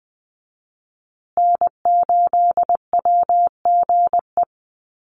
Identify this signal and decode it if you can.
Morse code, heard as a keyed tone, about 20 wpm, 700 Hz. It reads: N8WGE
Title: USA Call Signs Repeated 3x 20wpm